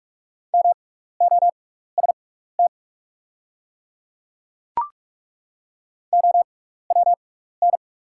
Morse code, heard as a keyed tone, about 42 wpm, spaced as for 15 wpm, 690 Hz. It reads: MOST OWN
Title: 6x Character Spacing - Top 300 Words - 45wpm